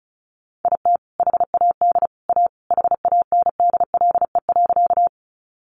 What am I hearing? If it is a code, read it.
Morse code, heard as a keyed tone, about 34 wpm, 700 Hz. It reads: IT HAD A HANDLE.
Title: Sentences from Top 1100 Words 35wpm